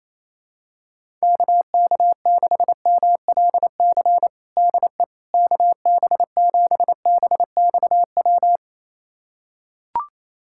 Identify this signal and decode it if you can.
Morse code, heard as a keyed tone, about 27 wpm, 700 Hz. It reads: KK6MLC DE KB7BXW
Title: USA Callsign DE Callsign 28wpm